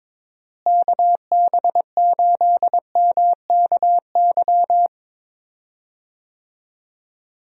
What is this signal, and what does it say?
Morse code, heard as a keyed tone, about 21 wpm, 700 Hz. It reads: KB8MKY
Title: USA Call Signs - 6 Characters - Repeated 3x - 22wpm